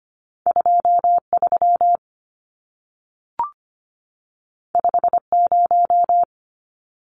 Morse code, heard as a keyed tone, about 25 wpm, 700 Hz. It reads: 23 50
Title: Numbers - 2-Digit 25wpm